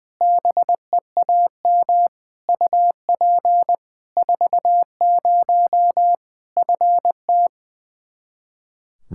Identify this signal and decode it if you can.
Morse code, heard as a keyed tone, about 21 wpm, 700 Hz. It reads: BEAM UP 40 FT
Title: Practice QSO - Exchange 2 - Info - 20wpm